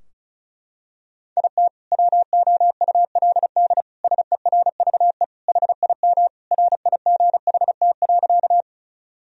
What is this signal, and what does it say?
Morse code, heard as a keyed tone, about 34 wpm, 700 Hz. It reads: IT WOULD SERVE HIM RIGHT.
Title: Sentences from Top 1000 Words - Review Part 2 - 35wpm